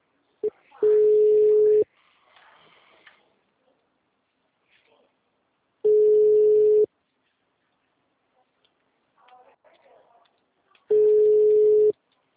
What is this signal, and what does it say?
Ringback tone (Europe, 425 Hz), with speech.